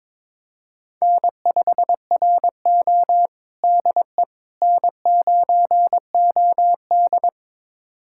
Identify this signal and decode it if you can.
Morse code, heard as a keyed tone, about 21 wpm, 700 Hz. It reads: N5RO DE N9OD
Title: USA Callsign DE Callsign - 4 Characters - 22wpm